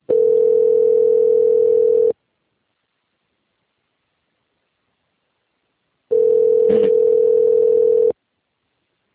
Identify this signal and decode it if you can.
Ringback tone (North America), 440 + 480 Hz.
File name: out- - - - - .69906